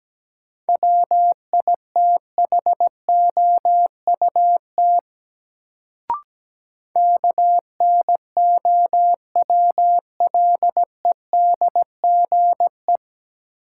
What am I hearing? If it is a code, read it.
Morse code, heard as a keyed tone, about 17 wpm, 700 Hz. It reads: WITHOUT KNOWLEDGE
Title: Sentences from Top 600 Words - New Word Drill - 17wpm